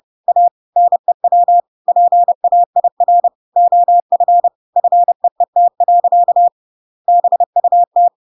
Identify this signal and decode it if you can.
Morse code, heard as a keyed tone, about 30 wpm, 700 Hz. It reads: A NEW PAIR OF FEET. BUT
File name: Pinocchio / Pinocchio ch8 30 30 ik1hge